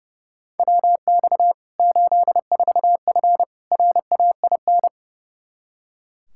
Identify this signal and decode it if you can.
Morse code, heard as a keyed tone, about 32 wpm, 700 Hz. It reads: WX 84F RAIN
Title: K1USN SST QSO Practice 30wpm